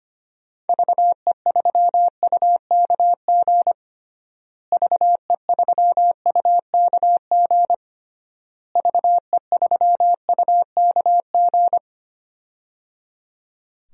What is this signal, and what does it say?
Morse code, heard as a keyed tone, about 25 wpm, 700 Hz. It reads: VE3UKG VE3UKG VE3UKG
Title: DXCC-100 Call Signs - 6 Characters - Repeated 3x - 25wpm